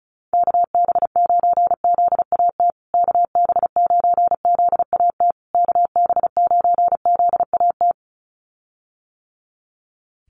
Morse code, heard as a keyed tone, about 34 wpm, 700 Hz. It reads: KB9ZAT KB9ZAT KB9ZAT
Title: USA Call Signs Repeated 3x 35wpm